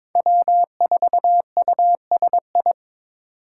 Morse code, heard as a keyed tone, about 21 wpm, 700 Hz. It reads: W4USI
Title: K1USN SST QSO Practice - 22wpm